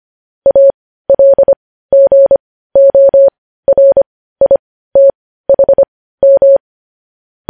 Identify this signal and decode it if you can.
Morse code, heard as a keyed tone, about 25 wpm, spaced as for 15 wpm, 550 Hz. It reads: ALGORITHM
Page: news-headlines-in-morse-code-at-15-wpm